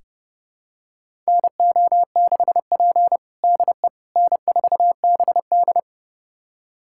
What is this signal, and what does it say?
Morse code, heard as a keyed tone, about 30 wpm, 700 Hz. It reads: NO6P DE N4BD
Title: USA Callsign DE Callsign - 4 Characters - 30wpm